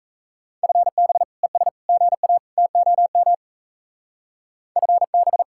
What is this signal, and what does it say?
Morse code, heard as a keyed tone, about 34 wpm, 700 Hz. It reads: FB ES GA TOM FB
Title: Speed-Racing Common QSO Phrases Encore - 28wpm